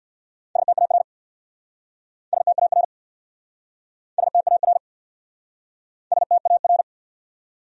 Morse code, heard as a keyed tone, about 67 wpm, 700 Hz. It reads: STAR STAR STAR STAR